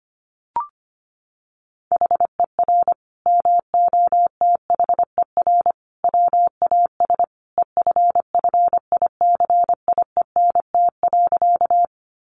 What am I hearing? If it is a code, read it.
Morse code, heard as a keyed tone, about 25 wpm, 700 Hz. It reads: HER MOTHER WAS EFFICIENT.
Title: Sentences from Top 1300 Words 25wpm